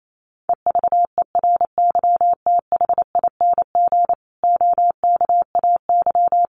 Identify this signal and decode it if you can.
Morse code, heard as a keyed tone, about 28 wpm, 700 Hz. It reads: EVERYTHING OKAY
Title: A Sound Of Thunder 28wpm